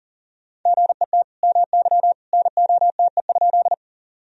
Morse code, heard as a keyed tone, about 40 wpm, 700 Hz. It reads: GET MY NOTE?
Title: Sentences from Top 1000 Words - Review Part 2 - 40wpm